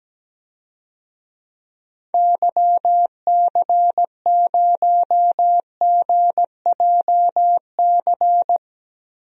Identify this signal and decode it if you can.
Morse code, heard as a keyed tone, about 17 wpm, 700 Hz. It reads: YC0GJC
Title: DXCC-100 Call Signs - 6 Characters - Repeated 3x - 17wpm